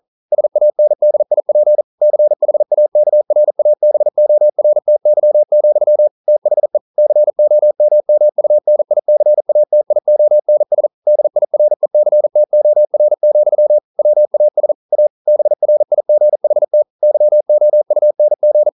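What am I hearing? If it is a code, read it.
Morse code, heard as a keyed tone, about 42 wpm, 600 Hz. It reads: SANDIP CHAKRABORTY, THE COMMUNICATIONS DIRECTOR, WAS A BRIGHT YOUNG
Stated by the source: Morse code practice at 41 wpm, from CW med Gnister.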